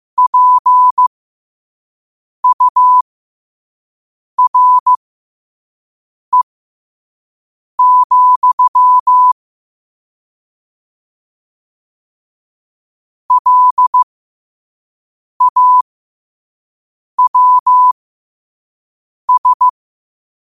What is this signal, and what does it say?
Morse code, heard as a keyed tone, about 15 wpm, spaced as for 5 wpm, 1000 Hz. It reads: PURE, LAWS
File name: qotdfiles / 5wpm